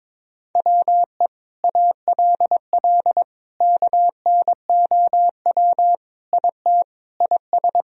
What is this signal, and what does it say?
Morse code, heard as a keyed tone, about 22 wpm, 700 Hz. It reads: WE ALL KNOW IT IS
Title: Sentences from Top 1000 Words - Review Part 1 - 22wpm